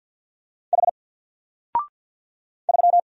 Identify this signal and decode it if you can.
Morse code, heard as a keyed tone, about 50 wpm, 700 Hz. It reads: H 3
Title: Single Character (Letter:Number:Punctuation:Prosign) 50wpm